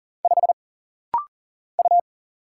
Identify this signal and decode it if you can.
Morse code, heard as a keyed tone, about 40 wpm, 700 Hz. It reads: 5 U
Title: Single Letter-Number Alt - Rapid-Fire 40wpm